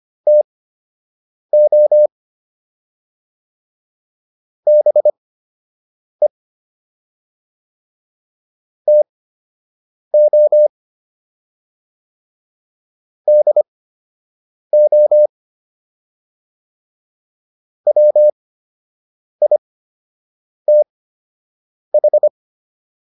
Morse code, heard as a keyed tone, about 25 wpm, spaced as for 7 wpm, 600 Hz. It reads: TO BE TO DO WITH